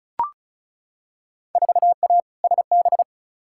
Morse code, heard as a keyed tone, about 36 wpm, 700 Hz. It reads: 4A SB
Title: Field Day Exchanges Encore - 35wpm